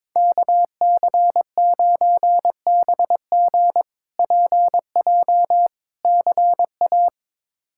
Morse code, heard as a keyed tone, about 22 wpm, 700 Hz. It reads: KC9BG PJ CA
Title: K1USN SST QSO Practice - 22wpm